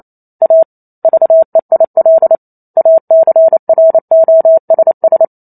Morse code, heard as a keyed tone, about 28 wpm, 650 Hz. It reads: A VEIL ACROSS
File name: Chapter 2 - 28 WPM